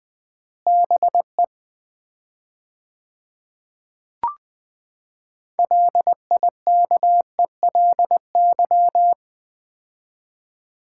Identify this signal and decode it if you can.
Morse code, heard as a keyed tone, about 20 wpm, 700 Hz. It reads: BE LIKELY
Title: 4x Word-Spacing - Sets of 2 Words - 20wpm